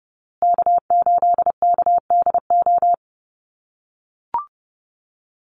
Morse code, heard as a keyed tone, about 30 wpm, 700 Hz. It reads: K8KDO
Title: USA Call Signs Repeated 2x 30wpm